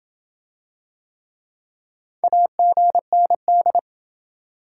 Morse code, heard as a keyed tone, about 28 wpm, 700 Hz. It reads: AGND